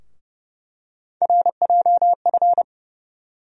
Morse code, heard as a keyed tone, about 30 wpm, 700 Hz. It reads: RJF
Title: USA Call Sign Postfixes - 3 Letters 30wpm